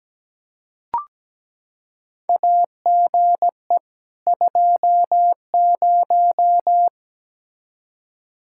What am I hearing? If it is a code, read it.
Morse code, heard as a keyed tone, about 17 wpm, 700 Hz. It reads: AGE 20
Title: Practice QSO - Exchange 3 - More Info - 17wpm